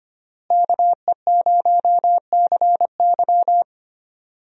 Morse code, heard as a keyed tone, about 25 wpm, 700 Hz. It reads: KE0CY
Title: USA Call Signs - 5 Characters - Repeated 2x - 25wpm